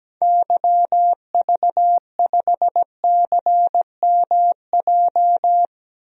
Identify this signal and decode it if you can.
Morse code, heard as a keyed tone, about 17 wpm, 700 Hz. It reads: YV5CMJ
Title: DXCC-100 Call Signs - 6 Characters - 17wpm